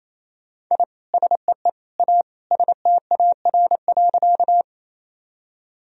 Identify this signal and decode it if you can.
Morse code, heard as a keyed tone, about 28 wpm, 700 Hz. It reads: I SEE A STAR.
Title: Sentences from Top 1000 Words - Review Part 2 - 28wpm